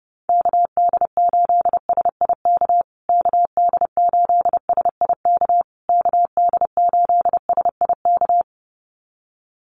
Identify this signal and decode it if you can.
Morse code, heard as a keyed tone, about 30 wpm, 700 Hz. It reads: KD8SIK KD8SIK KD8SIK
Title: USA Call Signs Repeated 3x 30wpm